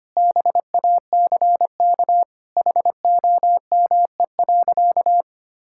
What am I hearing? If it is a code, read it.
Morse code, heard as a keyed tone, about 25 wpm, 700 Hz. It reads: BACK HOME.
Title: Sentences from Top 1000 Words - Review Part 6 - 25wpm